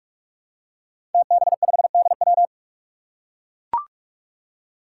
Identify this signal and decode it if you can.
Morse code, heard as a keyed tone, about 44 wpm, 700 Hz. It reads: TB5DW
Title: Five Challenging Letter-Number Combinations 45wpm